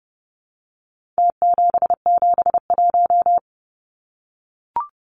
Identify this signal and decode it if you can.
Morse code, heard as a keyed tone, about 30 wpm, 700 Hz. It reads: T771
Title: Numbers - Contest Style - 4-Digit 30wpm